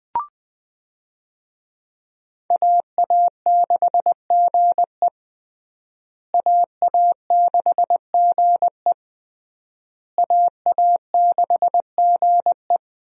Morse code, heard as a keyed tone, about 21 wpm, 700 Hz. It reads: AA6GE AA6GE AA6GE
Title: USA Call Signs - 5 Characters - Repeated 3x - 20wpm